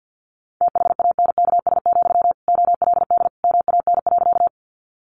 Morse code, heard as a keyed tone, about 50 wpm, 700 Hz. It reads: THANKS, OLD MAN.
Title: Sentences from Top 600 Words 50wpm